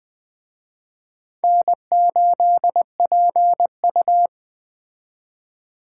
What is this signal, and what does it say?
Morse code, heard as a keyed tone, about 20 wpm, 700 Hz. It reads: N8PU
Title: Speed Racing - USA 1x2 Call Signs 17wpm